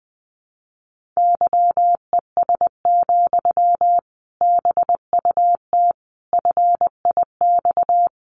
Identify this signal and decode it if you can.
Morse code, heard as a keyed tone, about 20 wpm, 700 Hz. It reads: YES, BUT FIX
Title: Sentences from Top 1400 Words 20wpm